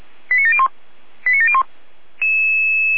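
Five-tone selective call (ZVEI), heard as quick runs of five tones, tones of about 70 ms. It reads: 89701 89701
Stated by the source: Five-tone ZVEI selective call 89715, received on SDR.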